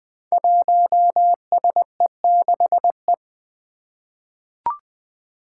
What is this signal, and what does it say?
Morse code, heard as a keyed tone, about 20 wpm, 700 Hz. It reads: 1SE6E
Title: Five Challenging Letter-Number Combinations 20wpm